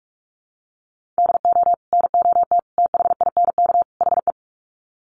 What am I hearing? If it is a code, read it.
Morse code, heard as a keyed tone, about 44 wpm, 700 Hz. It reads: DO NOT THINK HE
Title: Sets of 4 Words 45wpm